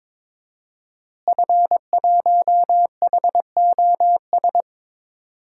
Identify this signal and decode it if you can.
Morse code, heard as a keyed tone, about 22 wpm, 700 Hz. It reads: F1HOS
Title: DXCC-100 Call Signs - 5 Characters - Repeated 2x - 22wpm